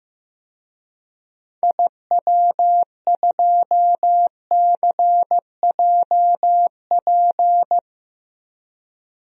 Morse code, heard as a keyed tone, about 15 wpm, 700 Hz. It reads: IW2CJP
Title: DXCC-100 Call Signs - 6 Characters - Repeated 3x - 15wpm